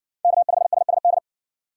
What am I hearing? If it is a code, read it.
Morse code, heard as a keyed tone, about 60 wpm, 700 Hz. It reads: K5SSD